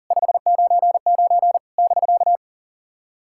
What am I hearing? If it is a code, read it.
Morse code, heard as a keyed tone, about 40 wpm, 700 Hz. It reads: 599 <BK>
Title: SOTA QSO Practice 40wpm